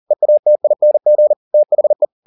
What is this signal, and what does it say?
Morse code, heard as a keyed tone, about 40 wpm, 600 Hz. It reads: EATING THE